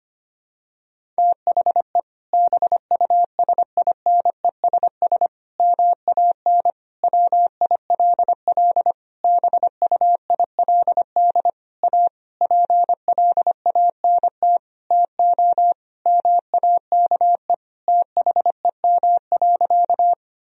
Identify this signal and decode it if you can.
Morse code, heard as a keyed tone, about 25 wpm, 700 Hz. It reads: THE BUSINESS MAN WILL BUILD A PLANT TO MAKE THEM.
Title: Sentences from Top 1000 Words - Review Part 1 - 25wpm